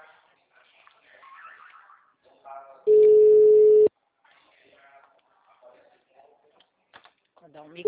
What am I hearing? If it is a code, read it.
Ringback tone (Europe, 425 Hz), with speech.